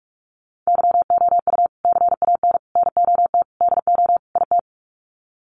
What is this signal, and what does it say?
Morse code, heard as a keyed tone, about 44 wpm, 700 Hz. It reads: YOU CAN NOT DO IT
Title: Sets of 5 Words Encore2 45wpm